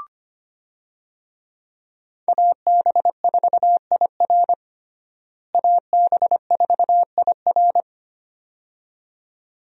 Morse code, heard as a keyed tone, about 25 wpm, 700 Hz. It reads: AB4IR AB4IR
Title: USA Call Signs - 5 Characters - Repeated 2x - 25wpm